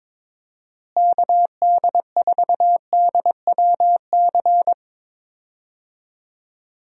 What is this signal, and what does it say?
Morse code, heard as a keyed tone, about 22 wpm, 700 Hz. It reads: KD4DWC
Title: USA Call Signs - 6 Characters - Repeated 3x - 22wpm